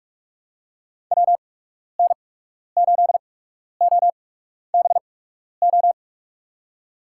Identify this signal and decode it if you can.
Morse code, heard as a keyed tone, about 46 wpm, spaced as for 12 wpm, 700 Hz. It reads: WN8OBO
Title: USA Call Signs - 6 Characters - 8x Character Spacing - 45wpm